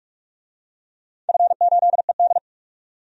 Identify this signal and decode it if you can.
Morse code, heard as a keyed tone, about 44 wpm, 700 Hz. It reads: F8ED
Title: DXCC-100 Call Signs - 4 Characters - 45wpm